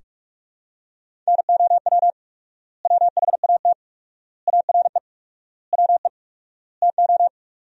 Morse code, heard as a keyed tone, about 41 wpm, 700 Hz. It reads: NOW WHAT ARE WE TO
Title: 4x Word-Spacing - Sentences from Top 100 Words - 45wpm